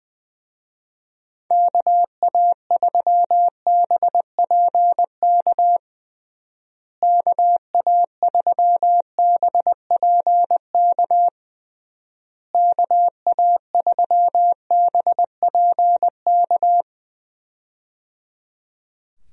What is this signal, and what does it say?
Morse code, heard as a keyed tone, about 21 wpm, 700 Hz. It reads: KA3BPK KA3BPK KA3BPK
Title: USA Call Signs - 6 Characters - Repeated 3x - 20wpm